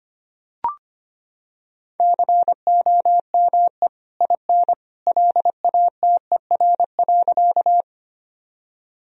Morse code, heard as a keyed tone, about 25 wpm, 700 Hz. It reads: COME IN LATER.
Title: Sentences from Top 1000 Words - Review Part 4 - 25wpm